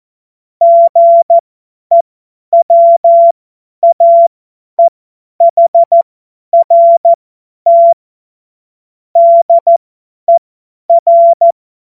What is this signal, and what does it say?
Morse code, heard as a keyed tone, about 14 wpm, spaced as for 10 wpm, 680 Hz. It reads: GEWAEHRT DER
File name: Morsekurs / Witze / Witz 20. Dezember 2022 50 F70